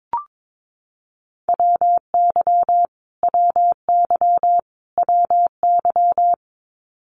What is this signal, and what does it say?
Morse code, heard as a keyed tone, about 22 wpm, 700 Hz. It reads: WY WY WY
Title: US States Repeated 3x 22wpm